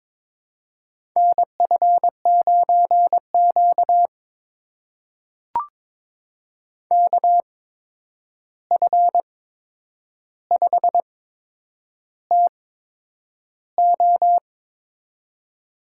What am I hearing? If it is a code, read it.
Morse code, heard as a keyed tone, about 23 wpm, spaced as for 9 wpm, 700 Hz. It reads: NF9Q KF5TO
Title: USA Call Signs - 8x Character Spacing - 22wpm